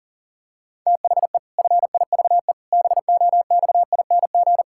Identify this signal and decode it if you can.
Morse code, heard as a keyed tone, about 40 wpm, 700 Hz. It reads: THE FIVE BOXING
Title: Pangram Challenge v2 - 40wpm